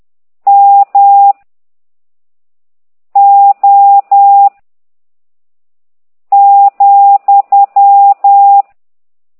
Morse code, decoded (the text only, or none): M O ,